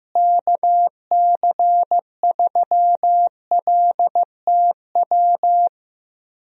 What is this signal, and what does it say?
Morse code, heard as a keyed tone, about 15 wpm, 700 Hz. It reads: KC3LTW
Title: USA Call Signs - 6 Characters - Repeated 3x - 15wpm